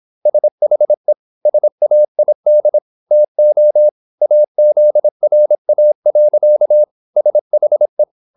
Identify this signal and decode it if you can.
Morse code, heard as a keyed tone, about 26 wpm, 600 Hz. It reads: SHE SAID TO AZRA. SHE
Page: Never 26wpm | CW med Gnister